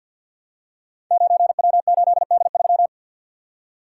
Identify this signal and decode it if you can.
Morse code, heard as a keyed tone, about 50 wpm, 690 Hz. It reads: 9W8D3
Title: Code Groups - 5 Letter:Numbers - 50wpm